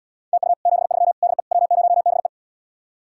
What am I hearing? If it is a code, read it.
Morse code, heard as a keyed tone, about 79 wpm, 700 Hz. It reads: TU 73 DE W0BE